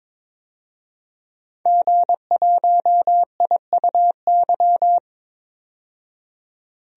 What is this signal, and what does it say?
Morse code, heard as a keyed tone, about 22 wpm, 700 Hz. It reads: G1IUY
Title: DXCC-100 Call Signs - 5 Characters - 22wpm